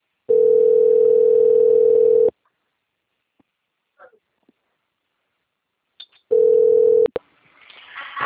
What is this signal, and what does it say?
Ringback tone (North America), 440 + 480 Hz.